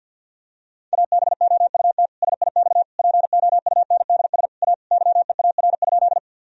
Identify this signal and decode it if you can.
Morse code, heard as a keyed tone, about 50 wpm, 700 Hz. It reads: ABOUT SIX POUNDS A YEAR?
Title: Sentences from Top 1000 Words - Review Part 2 - 50wpm